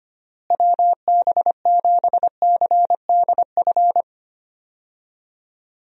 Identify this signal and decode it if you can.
Morse code, heard as a keyed tone, about 25 wpm, 700 Hz. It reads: WB7CDF
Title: USA Call Signs - 6 Characters - Repeated 3x - 25wpm